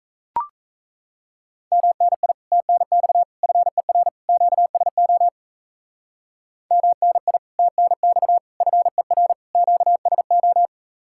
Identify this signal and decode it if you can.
Morse code, heard as a keyed tone, about 36 wpm, 700 Hz. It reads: MNI TNX FER QSO MNI TNX FER QSO